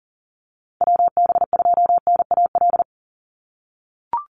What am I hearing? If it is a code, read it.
Morse code, heard as a keyed tone, about 37 wpm, 700 Hz. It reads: WB2NAL
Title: USA Call Signs 40wpm